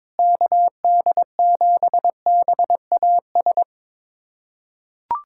Morse code, heard as a keyed tone, about 22 wpm, 700 Hz. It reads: KD7BAS
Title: USA Callsign DE Callsign - 6 Characters - 22wpm